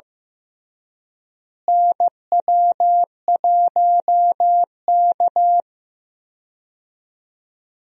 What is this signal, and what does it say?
Morse code, heard as a keyed tone, about 16 wpm, 700 Hz. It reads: NW1K
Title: USA Call Signs - 4 Characters - Repeated 3x - 15wpm